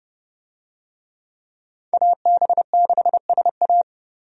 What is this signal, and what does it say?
Morse code, heard as a keyed tone, about 30 wpm, 700 Hz. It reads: AB6SA
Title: USA Call Signs - 5 Characters - 30wpm